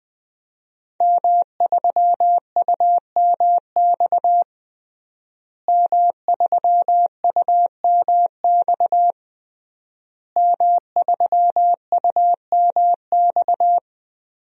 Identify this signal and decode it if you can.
Morse code, heard as a keyed tone, about 21 wpm, 700 Hz. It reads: M3UMX M3UMX M3UMX
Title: DXCC-100 Call Signs - 5 Characters - Repeated 3x - 20wpm